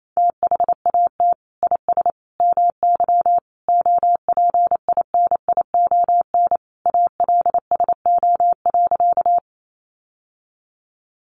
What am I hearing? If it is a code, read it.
Morse code, heard as a keyed tone, about 28 wpm, 700 Hz. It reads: THAT IS MY OPINION ALSO.
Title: Sentences from Top 1200 Words 28wpm